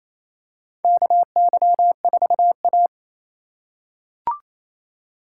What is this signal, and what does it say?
Morse code, heard as a keyed tone, about 27 wpm, 700 Hz. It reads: KY4A
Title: USA Call Signs - 4 Characters - 4x Character Spacing - 28wpm